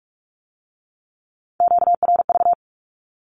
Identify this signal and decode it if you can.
Morse code, heard as a keyed tone, about 46 wpm, 700 Hz. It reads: QRV
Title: Q-Codes with Spoken Meaning - 45wpm